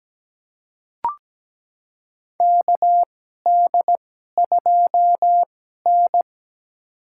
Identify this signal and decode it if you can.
Morse code, heard as a keyed tone, about 17 wpm, spaced as for 12 wpm, 700 Hz. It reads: KD2N
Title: USA Call Signs - 4 Characters - 2x Character Spacing - 17wpm